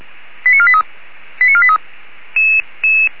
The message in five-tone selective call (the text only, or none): 89483 89483